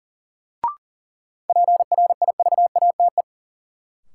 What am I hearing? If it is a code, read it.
Morse code, heard as a keyed tone, about 40 wpm, 700 Hz. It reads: PRIVATE
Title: Sentences from Top 1000 Words - New Word Drill - 40wpm